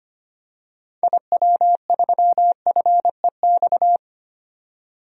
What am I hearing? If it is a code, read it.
Morse code, heard as a keyed tone, about 25 wpm, 700 Hz. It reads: IW3FEX
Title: DXCC-100 Call Signs - 6 Characters - 25wpm